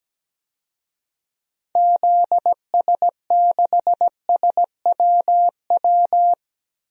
Morse code, heard as a keyed tone, about 17 wpm, 700 Hz. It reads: ZS6SWW
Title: DXCC-100 Call Signs - 6 Characters - 17wpm